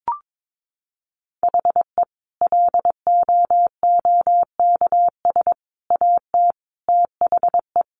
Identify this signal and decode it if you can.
Morse code, heard as a keyed tone, about 22 wpm, 700 Hz. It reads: HE LOOKS AT THE
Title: Additional Easy Phrases 22wpm